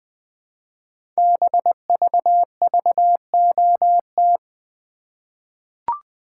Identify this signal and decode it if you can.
Morse code, heard as a keyed tone, about 20 wpm, 700 Hz. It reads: BVVOT
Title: Code Groups - 5 Letter:Numbers - 20wpm